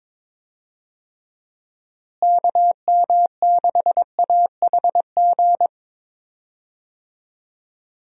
Morse code, heard as a keyed tone, about 22 wpm, 700 Hz. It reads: KM6AHG
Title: USA Call Signs - 6 Characters - Repeated 2x - 22wpm